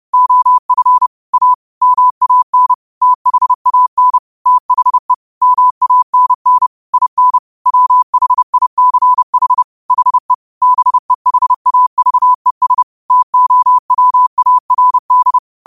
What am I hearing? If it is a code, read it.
Morse code, heard as a keyed tone, about 30 wpm, 1000 Hz. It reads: OF A MAN THAN THE MANN IN WHICH HE BEHAVES TOWARD
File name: qotdfiles / 30wpm